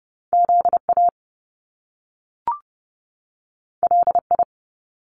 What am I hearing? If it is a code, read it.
Morse code, heard as a keyed tone, about 30 wpm, 700 Hz. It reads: ZA LI
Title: Top Two Letter Combinations in Words 30wpm